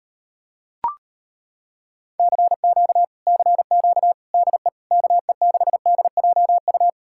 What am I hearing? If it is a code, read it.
Morse code, heard as a keyed tone, about 33 wpm, 700 Hz. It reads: CQ CQ DE KE6DJU